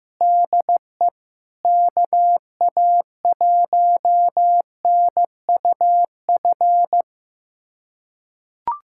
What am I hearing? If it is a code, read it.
Morse code, heard as a keyed tone, about 16 wpm, 700 Hz. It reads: DE KA1NUF
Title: USA Callsign DE Callsign - 6 Characters - 15wpm